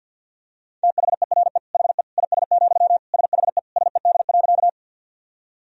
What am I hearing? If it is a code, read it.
Morse code, heard as a keyed tone, about 52 wpm, 700 Hz. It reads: THERE HE IS, SHE SED.
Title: Sentences from Top 100 Words and Abbreviations 50wpm